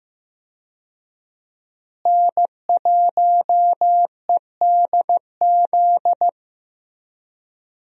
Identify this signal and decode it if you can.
Morse code, heard as a keyed tone, about 15 wpm, 700 Hz. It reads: N1EDZ
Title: USA Call Signs - 5 Characters - Repeated 3x - 15wpm